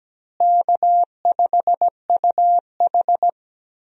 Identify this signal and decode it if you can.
Morse code, heard as a keyed tone, about 17 wpm, 700 Hz. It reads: K5UH
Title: USA Call Signs - 4 Characters - Repeated 2x - 17wpm